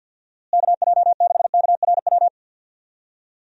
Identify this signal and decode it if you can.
Morse code, heard as a keyed tone, about 50 wpm, 700 Hz. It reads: KJ6KRW
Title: USA Call Signs - 6 Characters - Repeated 3x - 50wpm